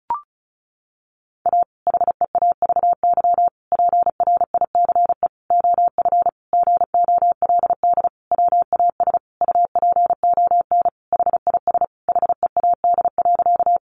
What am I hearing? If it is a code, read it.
Morse code, heard as a keyed tone, about 34 wpm, 700 Hz. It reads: A HEAVY PRICE OF GOLD WAS UPON HIS HEAD.
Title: Sentences from Top 1100 Words 35wpm